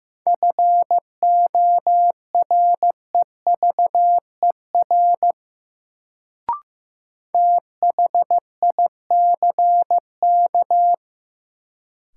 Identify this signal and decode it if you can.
Morse code, heard as a keyed tone, about 15 wpm, 700 Hz. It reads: FOREVER THICK
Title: Sentences from Top 1600 Words - New Word Drill Alt - 15wpm